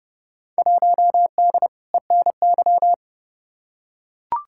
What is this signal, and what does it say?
Morse code, heard as a keyed tone, about 30 wpm, 700 Hz. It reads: 1D ENY
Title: Field Day Exchanges - 30wpm